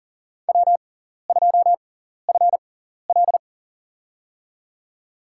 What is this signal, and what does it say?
Morse code, heard as a keyed tone, about 40 wpm, spaced as for 14 wpm, 700 Hz. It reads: W2FL
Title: USA Call Signs - 4 Characters - 6x Character Spacing - 40wpm